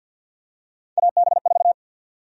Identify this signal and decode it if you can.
Morse code, heard as a keyed tone, about 52 wpm, 700 Hz. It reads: AB4
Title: USA Call Sign Prefixes - Repeated 2x - 50wpm